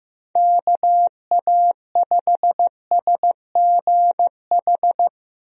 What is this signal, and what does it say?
Morse code, heard as a keyed tone, about 15 wpm, 700 Hz. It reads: KA5SGH
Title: USA Call Signs - 6 Characters - Repeated 2x - 15wpm